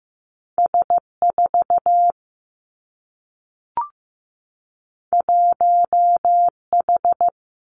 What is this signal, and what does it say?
Morse code, heard as a keyed tone, about 15 wpm, 700 Hz. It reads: S 4 1 H
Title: Two Challenging Letter-Number Combinations 15wpm